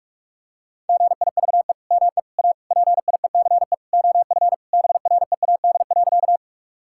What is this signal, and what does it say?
Morse code, heard as a keyed tone, about 46 wpm, 700 Hz. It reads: GIVE ME A PIECE OF BREAD.
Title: Sentences from Top 1600 Words 45wpm